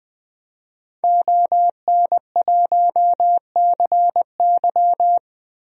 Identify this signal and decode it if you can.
Morse code, heard as a keyed tone, about 20 wpm, 700 Hz. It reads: ON1CY
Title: DXCC-100 Call Signs - 5 Characters - 20wpm